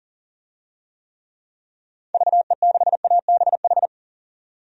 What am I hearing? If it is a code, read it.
Morse code, heard as a keyed tone, about 40 wpm, 700 Hz. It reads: VE6ABH
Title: DXCC-100 Call Signs - 6 Characters - 40wpm